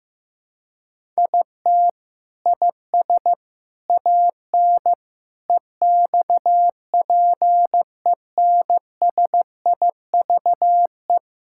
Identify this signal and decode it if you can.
Morse code, heard as a keyed tone, about 15 wpm, 700 Hz. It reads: IT IS AN EXPENSIVE
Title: Sentences from Top 1000 Words - Review Part 6 - 15wpm